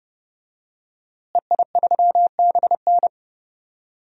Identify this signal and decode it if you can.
Morse code, heard as a keyed tone, about 30 wpm, 700 Hz. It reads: EI3BN
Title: DXCC-100 Call Signs - 5 Characters - Repeated 3x - 30wpm